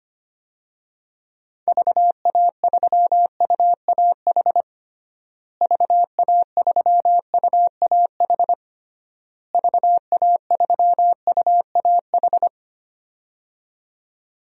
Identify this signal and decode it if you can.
Morse code, heard as a keyed tone, about 25 wpm, 700 Hz. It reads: VA3UAH VA3UAH VA3UAH
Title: DXCC-100 Call Signs - 6 Characters - Repeated 3x - 25wpm